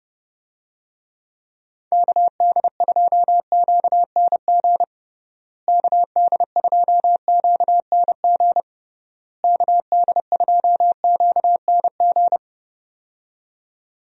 Morse code, heard as a keyed tone, about 30 wpm, 700 Hz. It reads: KD2QNG KD2QNG KD2QNG
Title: USA Call Signs - 6 Characters - Repeated 3x - 30wpm